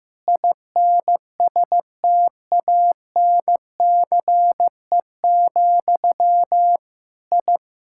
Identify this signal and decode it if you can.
Morse code, heard as a keyed tone, about 16 wpm, 700 Hz. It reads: INSTANCE, I
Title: Sentences from Top 1000 Words - Review Part 5 - 15wpm